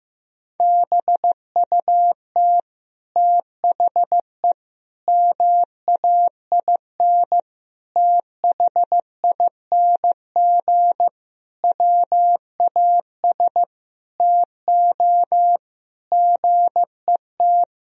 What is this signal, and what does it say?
Morse code, heard as a keyed tone, about 15 wpm, 700 Hz. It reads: BUT THE MAIN THING WAS TO GET
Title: Sentences from Top 1000 Words - Review Part 2 - 15wpm